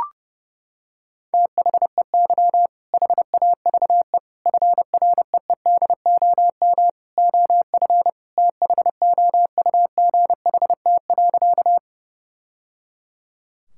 Morse code, heard as a keyed tone, about 31 wpm, 700 Hz. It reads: THEY HAVE FREEDOM OF THOUGHT.
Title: Sentences from Top 1000 Words - Review Part 5 - 30wpm